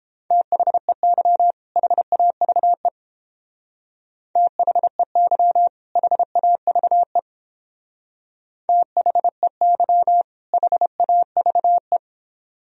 Morse code, heard as a keyed tone, about 29 wpm, 700 Hz. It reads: THEY HAVE THEY HAVE THEY HAVE